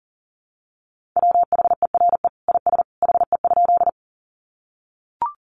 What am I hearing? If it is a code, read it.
Morse code, heard as a keyed tone, about 40 wpm, 700 Hz. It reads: WHERE IS HE?
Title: Sentences from Top 200 Words 40wpm